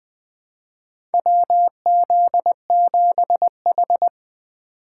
Morse code, heard as a keyed tone, about 20 wpm, 700 Hz. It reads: WZ7H
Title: USA Call Signs - 4 Characters - 3x Character Spacing - 20wpm